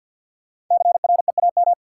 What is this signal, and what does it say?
Morse code, heard as a keyed tone, about 53 wpm, 700 Hz. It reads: CREAM
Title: Sentences from Top 1500 Words - New Word Drill - 50wpm